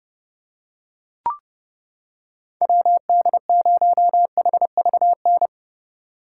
Morse code, heard as a keyed tone, about 29 wpm, 700 Hz. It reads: WD0HVN